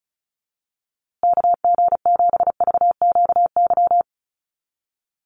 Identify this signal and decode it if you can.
Morse code, heard as a keyed tone, about 34 wpm, 700 Hz. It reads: KG7VQY
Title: USA Call Signs Repeated 3x 35wpm